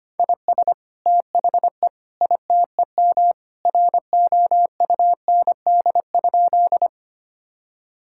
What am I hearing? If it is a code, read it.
Morse code, heard as a keyed tone, about 25 wpm, 700 Hz. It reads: IS THE ITEM ROUND?
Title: Sentences from Top 1000 Words - Review Part 6 - 25wpm